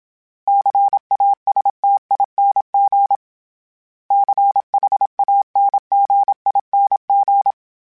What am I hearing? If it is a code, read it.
Morse code, heard as a keyed tone, about 27 wpm, 800 Hz. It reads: CASTING CHANGING
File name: Special INGs-mp3s / Special INGs-mp3s / Sngl Ing @25